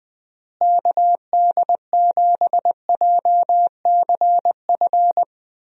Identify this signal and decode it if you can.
Morse code, heard as a keyed tone, about 20 wpm, 700 Hz. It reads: KD7JCF
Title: USA Call Signs - 6 Characters - Repeated 2x - 20wpm